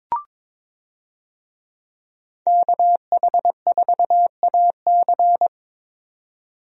Morse code, heard as a keyed tone, about 22 wpm, 700 Hz. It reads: KH4AC
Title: USA Call Signs - 5 Characters - 22wpm